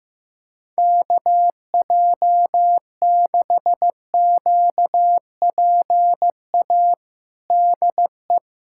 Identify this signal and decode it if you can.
Morse code, heard as a keyed tone, about 15 wpm, 700 Hz. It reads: KJ6QPA DE
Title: USA Callsign DE Callsign - 6 Characters - 15wpm